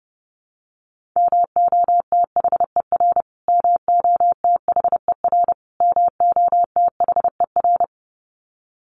Morse code, heard as a keyed tone, about 30 wpm, 700 Hz. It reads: MOTHER MOTHER MOTHER
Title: Top 200 Words Repeated 30wpm